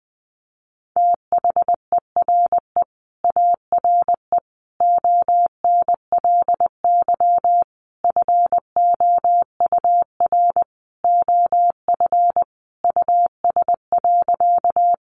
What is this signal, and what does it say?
Morse code, heard as a keyed tone, about 20 wpm, 700 Hz. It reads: THERE ARE ONLY FOUR OF US.
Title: Sentences from Top 300 Words 20wpm